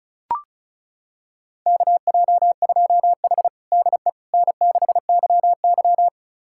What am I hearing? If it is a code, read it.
Morse code, heard as a keyed tone, about 36 wpm, 700 Hz. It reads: KJ2H DE N6YY
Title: USA Callsign DE Callsign - 4 Characters - 35wpm